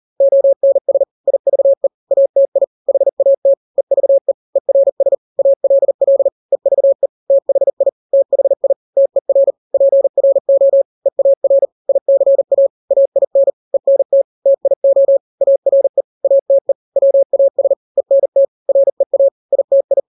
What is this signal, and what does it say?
Morse code, heard as a keyed tone, about 40 wpm, 550 Hz. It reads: ONS IVE ATI HAT EVE ERS ALL EVE THI THI TER PRO EAR ICA AIN ENT TIO ARE ATE WAS ENT REA ITI
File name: n-grams / 3-grams 40 35 WPM